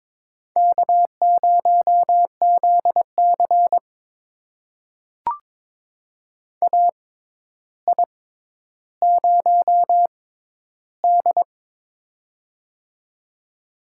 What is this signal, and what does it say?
Morse code, heard as a keyed tone, about 22 wpm, spaced as for 11 wpm, 700 Hz. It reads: K0ZC AI0D
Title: USA Call Signs - 4 Characters - 6x Character Spacing - 22wpm